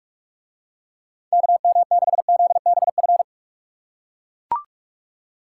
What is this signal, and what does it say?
Morse code, heard as a keyed tone, about 43 wpm, 700 Hz. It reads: KM6ZBF